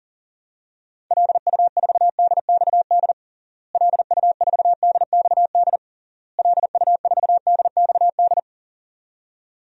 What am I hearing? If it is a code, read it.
Morse code, heard as a keyed tone, about 40 wpm, 700 Hz. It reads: LU4DXD LU4DXD LU4DXD
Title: DXCC-100 Call Signs - 6 Characters - Repeated 3x - 40wpm